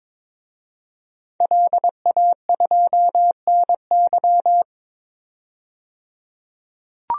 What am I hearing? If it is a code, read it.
Morse code, heard as a keyed tone, about 22 wpm, 700 Hz. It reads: LA2NY
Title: DXCC-100 Call Signs - 5 Characters - 22wpm